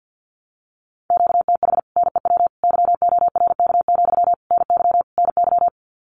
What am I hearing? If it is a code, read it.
Morse code, heard as a keyed tone, about 50 wpm, 700 Hz. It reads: QTH NEW YORK, NY NY
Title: Common QSO Phrases Encore 50wpm